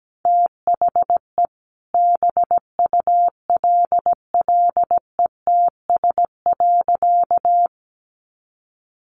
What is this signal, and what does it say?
Morse code, heard as a keyed tone, about 17 wpm, 700 Hz. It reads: THE BULLETS.
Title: A Sound Of Thunder 17wpm